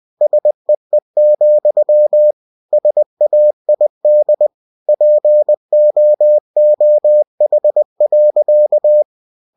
Morse code, heard as a keyed tone, about 20 wpm, 600 Hz. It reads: SEE, SAID POOH.